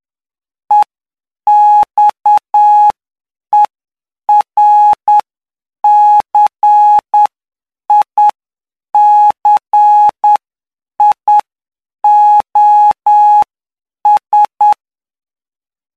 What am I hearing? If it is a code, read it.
Morse code, heard as a keyed tone, about 9 wpm, spaced as for 7 wpm, 840 Hz. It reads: EXERCICIOS
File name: cw / py2lcd / 11-PY2LCD-Li��o 20